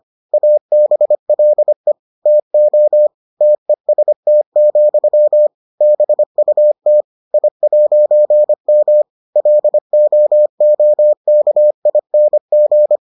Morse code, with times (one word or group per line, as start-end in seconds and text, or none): 0.3-1.9 ABLE
2.3-3.1 TO
3.4-5.5 TEST,
5.8-7.0 BUT
7.3-9.0 I'M
9.4-13.0 LOOKING